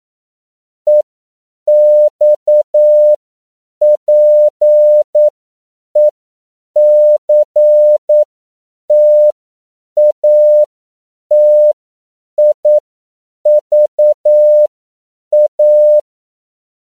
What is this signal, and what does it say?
Morse code, heard as a keyed tone, about 9 wpm, spaced as for 7 wpm, 600 Hz. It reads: EXPECTATIVA